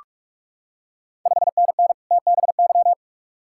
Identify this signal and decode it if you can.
Morse code, heard as a keyed tone, about 41 wpm, 690 Hz. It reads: 5NN TBY
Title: Ontario QSO Party Exchange 45wpm